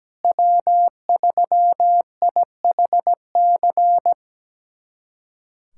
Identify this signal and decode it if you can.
Morse code, heard as a keyed tone, about 17 wpm, 700 Hz. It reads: W3IHC
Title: USA Callsign DE Callsign 17wpm